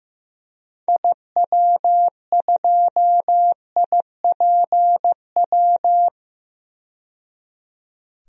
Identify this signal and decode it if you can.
Morse code, heard as a keyed tone, about 15 wpm, 700 Hz. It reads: IW2IPW
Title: DXCC-100 Call Signs - 6 Characters - 15wpm